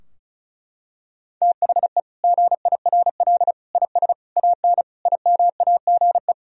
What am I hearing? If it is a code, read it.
Morse code, heard as a keyed tone, about 34 wpm, 700 Hz. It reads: THE GIRL IS AN IMAGE
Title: Sentences from Top 1000 Words - Review Part 4 - 35wpm